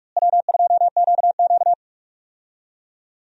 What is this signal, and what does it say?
Morse code, heard as a keyed tone, about 41 wpm, 700 Hz. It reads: W2QQ
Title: USA Call Signs - 4 Characters - Repeated 3x - 45wpm